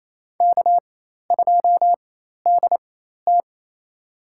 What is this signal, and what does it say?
Morse code, heard as a keyed tone, about 27 wpm, spaced as for 13 wpm, 700 Hz. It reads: K2DT
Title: USA Call Signs - 4 Characters - 4x Character Spacing - 28wpm